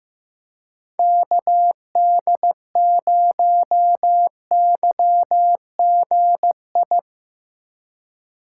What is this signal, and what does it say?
Morse code, heard as a keyed tone, about 17 wpm, 700 Hz. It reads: KD0YGI
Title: Speed Racing - USA 2x3 Call Signs 15wpm